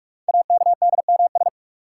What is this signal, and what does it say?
Morse code, heard as a keyed tone, about 44 wpm, 690 Hz. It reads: AKDMS
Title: Code Groups - 5 Letters - 45wpm